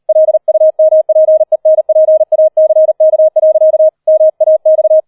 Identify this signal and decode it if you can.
Morse code, heard as a keyed tone, about 38 wpm, 630 Hz. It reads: LUMPENPACK. MAX